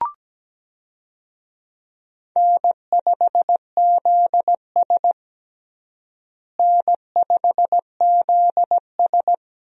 Morse code, heard as a keyed tone, about 17 wpm, 700 Hz. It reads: N5ZS N5ZS
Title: USA Call Signs - 4 Characters - Repeated 2x - 17wpm